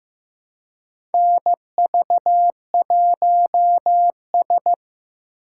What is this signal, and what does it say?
Morse code, heard as a keyed tone, about 15 wpm, 700 Hz. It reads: NV1S
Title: USA Call Signs - 4 Characters - 2x Character Spacing - 15wpm